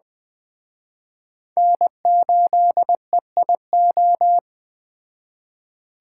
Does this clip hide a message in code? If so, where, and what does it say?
Yes: Morse code from 1.6 to 4.4 s, heard as a keyed tone, about 21 wpm, 700 Hz. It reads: N8EIO